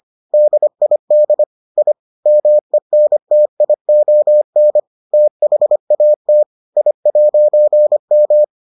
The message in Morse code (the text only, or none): DID I MENTION THAT I'M